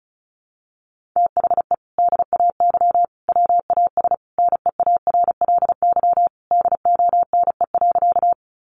Morse code, heard as a keyed tone, about 34 wpm, 700 Hz. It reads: THE DAY WAS NEARLY DONE.
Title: Sentences from Top 1300 Words 35wpm